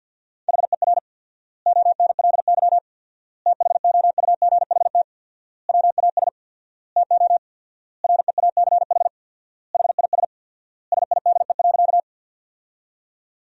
Morse code, heard as a keyed tone, about 50 wpm, 700 Hz. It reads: HER ONLY THOUGHT WAS TO REACH HIS SIDE.
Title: 4x Word-Spacing - Sentences from Top 400 Words - 50wpm